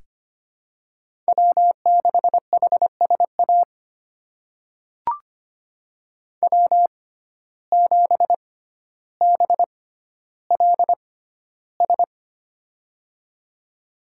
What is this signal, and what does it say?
Morse code, heard as a keyed tone, about 25 wpm, spaced as for 13 wpm, 700 Hz. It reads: W6HSA W7BLS
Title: USA Call Signs - 5 Characters - 6x Character Spacing - 25wpm